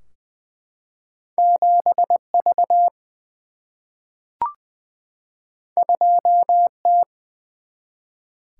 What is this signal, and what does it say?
Morse code, heard as a keyed tone, about 20 wpm, 700 Hz. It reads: 7 V 2 T
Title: Single-Letter Single-Number 20wpm